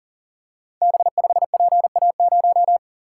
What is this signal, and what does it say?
Morse code, heard as a keyed tone, about 39 wpm, 700 Hz. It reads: B5PA0
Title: Code Groups - 5 Letter:Numbers - 40wpm